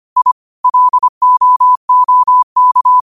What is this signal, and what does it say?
Morse code, heard as a keyed tone, about 25 wpm, 1000 Hz. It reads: I LOOK
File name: qotdfiles / 25wpm